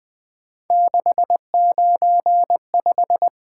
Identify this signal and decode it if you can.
Morse code, heard as a keyed tone, about 20 wpm, 700 Hz. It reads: 695
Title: Three Letter-Number Combinations - 20wpm